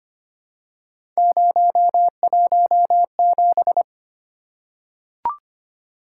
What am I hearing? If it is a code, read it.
Morse code, heard as a keyed tone, about 25 wpm, 700 Hz. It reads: 017
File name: Three Letter-Number Combinations - 25wpm